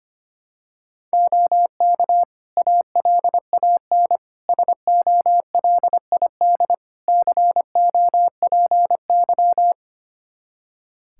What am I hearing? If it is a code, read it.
Morse code, heard as a keyed tone, about 25 wpm, 700 Hz. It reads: OK ALAN SOLID COPY
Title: Practice QSO - Exchange 3 - More Info - 25wpm